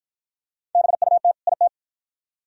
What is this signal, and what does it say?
Morse code, heard as a keyed tone, about 53 wpm, 700 Hz. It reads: BUT IT